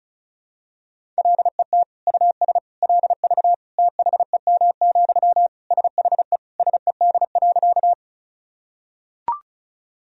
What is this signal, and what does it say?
Morse code, heard as a keyed tone, about 36 wpm, 700 Hz. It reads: LET US LV THEM, SHE SED.
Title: Sentences from Top 400 Words and Abbreviations 35wpm